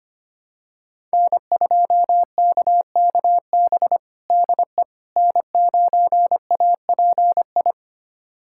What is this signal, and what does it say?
Morse code, heard as a keyed tone, about 25 wpm, 700 Hz. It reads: N2KKB DE N9API
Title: USA Callsign DE Callsign - 5 Characters - 25wpm